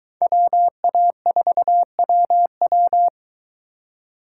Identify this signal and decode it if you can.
Morse code, heard as a keyed tone, about 23 wpm, 700 Hz. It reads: WA4WW